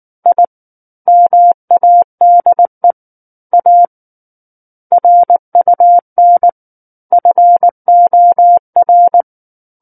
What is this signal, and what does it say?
Morse code, heard as a keyed tone, about 19 wpm, 700 Hz. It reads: I MADE A RUN FOR
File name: Book One / 10-Chapter 10 - 19 WPM